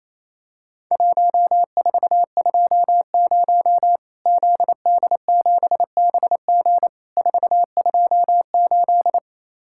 Morse code, heard as a keyed tone, about 28 wpm, 700 Hz. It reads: 1420 ZD7BG 428
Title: Worked All Europe DX Contest - QTC Exchange - 28wpm